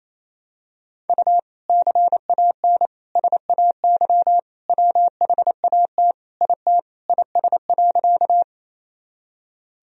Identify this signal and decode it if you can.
Morse code, heard as a keyed tone, about 27 wpm, 700 Hz. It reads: U CAN SAY WHAT IT IS.
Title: Sentences from Top 200 Words and Abbreviations 28wpm